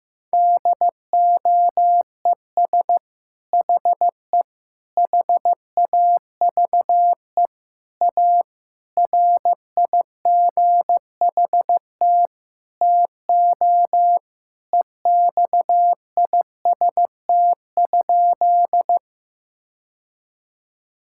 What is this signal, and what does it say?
Morse code, heard as a keyed tone, about 15 wpm, 700 Hz. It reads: DOES HE HAVE A RIGHT TO EXIST?
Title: Sentences from Top 1000 Words - Review Part 5 - 15wpm